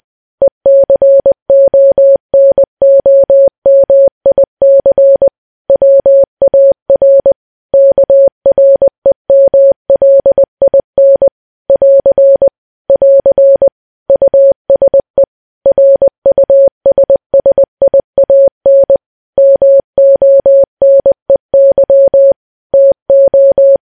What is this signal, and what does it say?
Morse code, heard as a keyed tone, about 20 wpm, 550 Hz. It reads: ECONOMIC WAR KREMLIN <AR> <AR> USE RUSSIAN MONEY TO
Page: news-headlines-in-morse-code-at-20-wpm